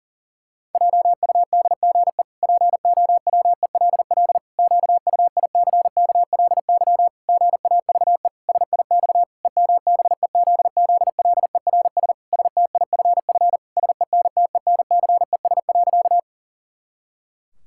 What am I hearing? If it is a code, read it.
Morse code, heard as a keyed tone, about 37 wpm, 700 Hz. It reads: JUDGE POWELL QUICKLY GAVE SIX EMBEZZLERS STIFF SENTENCES.
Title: Pangram Challenge v2 - 40wpm